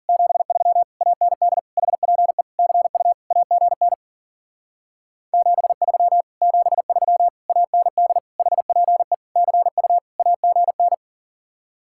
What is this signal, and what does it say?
Morse code, heard as a keyed tone, about 52 wpm, 700 Hz. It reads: 73 AND HPE CU AGN 73 73 AND HPE CU AGN